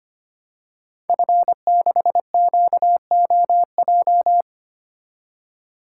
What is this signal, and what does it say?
Morse code, heard as a keyed tone, about 25 wpm, 700 Hz. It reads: F6QOJ
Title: Code Groups - 5 Letter:Numbers - 25wpm